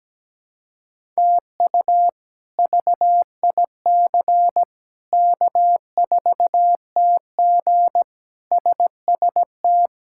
Morse code, heard as a keyed tone, about 17 wpm, 700 Hz. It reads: TU VIC K4TG SST
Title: K1USN SST QSO Practice - 17wpm